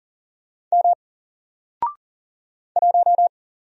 Morse code, heard as a keyed tone, about 41 wpm, 700 Hz. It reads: M 1
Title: Single Letter-Number Alt 40wpm